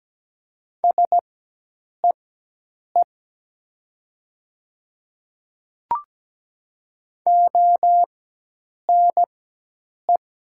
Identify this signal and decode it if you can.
Morse code, heard as a keyed tone, about 17 wpm, spaced as for 8 wpm, 700 Hz. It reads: SEE ONE
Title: 4x Character Spacing - Top 100 Words - 17wpm